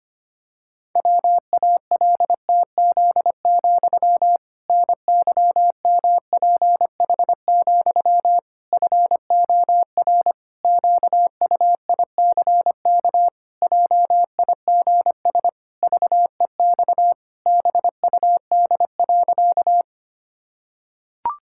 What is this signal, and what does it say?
Morse code, heard as a keyed tone, about 25 wpm, 700 Hz. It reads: WALTZ, NYMPH, FOR QUICK JIGS VEX BUD.
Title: Pangram Challenge v2 - 25wpm